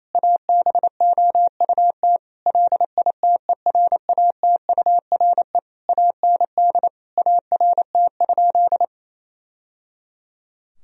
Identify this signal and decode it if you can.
Morse code, heard as a keyed tone, about 28 wpm, 700 Hz. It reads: ABOUT LITERATURE AND ART?
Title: Sentences from Top 1000 Words - Review Part 3 - 28wpm